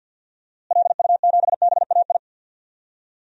Morse code, heard as a keyed tone, about 52 wpm, 700 Hz. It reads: LU7BAI
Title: DXCC-100 Call Signs - 6 Characters - Repeated 2x - 50wpm